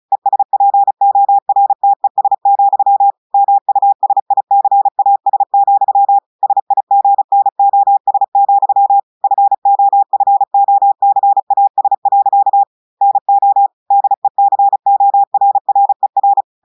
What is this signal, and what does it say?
Morse code, heard as a keyed tone, about 36 wpm, 800 Hz. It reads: ESPORTES, MUSICAS, SIGNOS, FOFOCAS. NO DECORRER
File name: audios / portugais / portugais texte en clair 2 vitesse 35 mots minutes